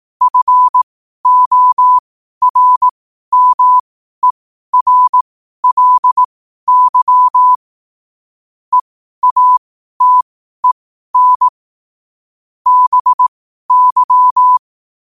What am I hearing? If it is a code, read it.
Morse code, heard as a keyed tone, about 18 wpm, spaced as for 12 wpm, 1000 Hz. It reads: FORMERLY EATEN BY
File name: qotdfiles / 12wpm